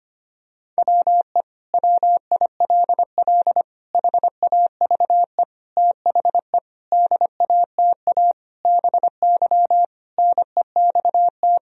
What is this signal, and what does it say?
Morse code, heard as a keyed tone, about 25 wpm, 700 Hz. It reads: WE WILL HAVE THE DATA BY NEXT
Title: Sentences from Top 1000 Words - Review Part 3 - 25wpm